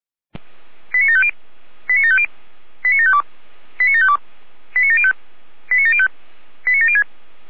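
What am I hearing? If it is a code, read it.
Five-tone selective call (ZVEI), heard as quick runs of five tones, tones of about 70 ms. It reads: 89750 89750 89752 89752 89705 89705 89706
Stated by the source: ZVEI five-tone selective call received by SDR.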